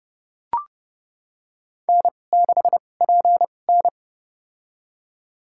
Morse code, heard as a keyed tone, about 30 wpm, spaced as for 22 wpm, 700 Hz. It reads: N6PN
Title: USA Call Signs - 4 Characters - 2x Character Spacing - 30wpm